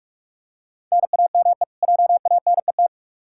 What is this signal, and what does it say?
Morse code, heard as a keyed tone, about 46 wpm, 700 Hz. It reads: NAME JANET
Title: K1USN SST QSO Practice 45wpm